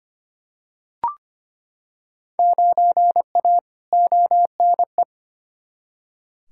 Morse code, heard as a keyed tone, about 25 wpm, 700 Hz. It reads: 9A ONE
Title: Field Day Exchanges Encore - 25wpm